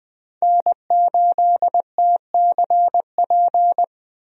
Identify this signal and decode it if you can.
Morse code, heard as a keyed tone, about 20 wpm, 700 Hz. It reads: N8TCP
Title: USA Call Signs - 5 Characters - 20wpm